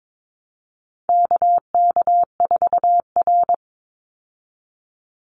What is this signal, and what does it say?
Morse code, heard as a keyed tone, about 22 wpm, 700 Hz. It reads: KK4R
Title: Call Signs Repeated 2x - 22wpm